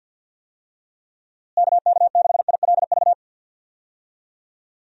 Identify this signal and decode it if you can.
Morse code, heard as a keyed tone, about 48 wpm, 700 Hz. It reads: KK6ILV
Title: USA Call Signs - 6 Characters - Repeated 2x - 50wpm